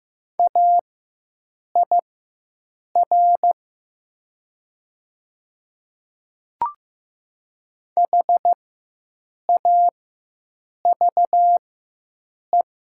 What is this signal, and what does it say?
Morse code, heard as a keyed tone, about 15 wpm, spaced as for 7 wpm, 700 Hz. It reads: AIR HAVE
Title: 4x Character Spacing - Top 300 Words - 15wpm